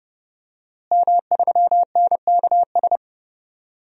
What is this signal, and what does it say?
Morse code, heard as a keyed tone, about 30 wpm, 700 Hz. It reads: M3NKS
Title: DXCC-100 Call Signs - 5 Characters - Repeated 3x - 30wpm